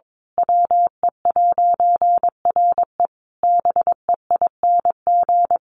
Morse code, heard as a keyed tone, about 22 wpm, 700 Hz. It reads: WE'RE BEING
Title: A Sound Of Thunder 22wpm